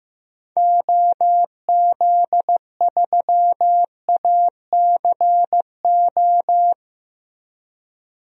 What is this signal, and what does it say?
Morse code, heard as a keyed tone, about 15 wpm, 700 Hz. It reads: OZ3ACO
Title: DXCC-100 Call Signs - 6 Characters - Repeated 3x - 15wpm